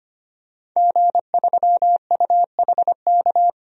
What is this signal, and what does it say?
Morse code, heard as a keyed tone, about 25 wpm, 700 Hz. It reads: G3UHK
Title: DXCC-100 Call Signs - 5 Characters - 25wpm